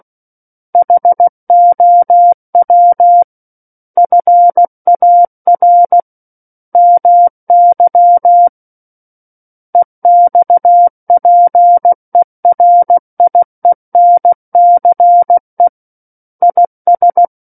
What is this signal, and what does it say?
Morse code, heard as a keyed tone, about 16 wpm, 700 Hz. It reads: HOW FAR MY EXPERIENCE IS
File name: Book One / 07-Chapter 7 - 16 WPM